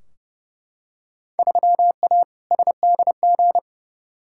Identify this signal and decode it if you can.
Morse code, heard as a keyed tone, about 31 wpm, 700 Hz. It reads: 3A SDG
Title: Field Day Exchanges - 30wpm